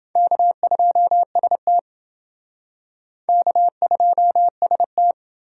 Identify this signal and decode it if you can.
Morse code, heard as a keyed tone, about 28 wpm, 700 Hz. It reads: K2ST K2ST